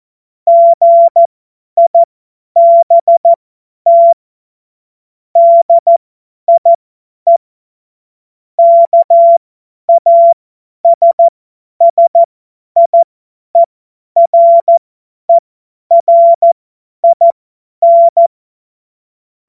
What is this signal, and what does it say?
Morse code, heard as a keyed tone, about 14 wpm, spaced as for 10 wpm, 680 Hz. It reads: GIBT DIE KASSIERERIN
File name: Morsekurs / Witze / Witz 20. Januar 2026 50 F70